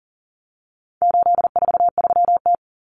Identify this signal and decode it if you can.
Morse code, heard as a keyed tone, about 40 wpm, 700 Hz. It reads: 843T
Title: Common Cut Numbers - 4-Digit 40wpm